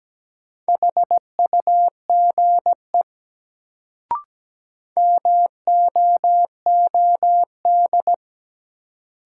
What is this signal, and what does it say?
Morse code, heard as a keyed tone, about 17 wpm, 700 Hz. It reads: HUGE MOOD
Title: Sentences from Top 1100 Words - New Word Drill Alt - 17wpm